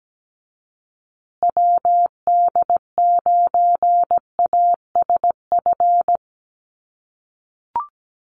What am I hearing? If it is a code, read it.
Morse code, heard as a keyed tone, about 17 wpm, 700 Hz. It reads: WD9ASF
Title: USA Call Signs Repeated 2x 17wpm